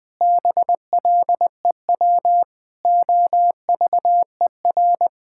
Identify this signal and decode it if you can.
Morse code, heard as a keyed tone, about 20 wpm, 700 Hz. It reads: BLEW OVER
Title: Pangram Challenge v2 - 20wpm